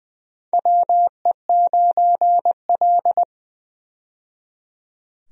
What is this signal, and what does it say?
Morse code, heard as a keyed tone, about 20 wpm, 700 Hz. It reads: WE9L
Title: USA Call Signs - 4 Characters - 20wpm